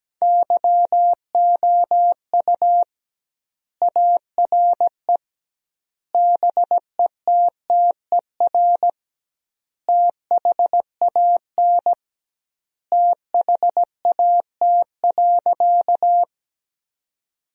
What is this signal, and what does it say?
Morse code, heard as a keyed tone, about 17 wpm, 700 Hz. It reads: YOU ARE BETTER THAN THAT.
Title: 2x Word-Spacing - Sentences from Top 400 Words - 17wpm